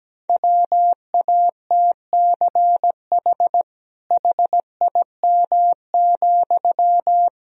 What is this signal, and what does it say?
Morse code, heard as a keyed tone, about 17 wpm, 700 Hz. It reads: WATCH HIM,
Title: Sentences from Top 300 Words and Abbreviations 17wpm